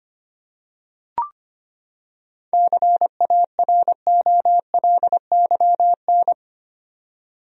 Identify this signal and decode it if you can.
Morse code, heard as a keyed tone, about 25 wpm, 700 Hz. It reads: CAROLYN
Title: USA Common Names - 25wpm